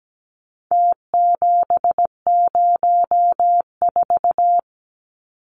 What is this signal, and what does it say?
Morse code, heard as a keyed tone, about 17 wpm, 700 Hz. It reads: T704
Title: Numbers - Contest Style - 4-Digit 17wpm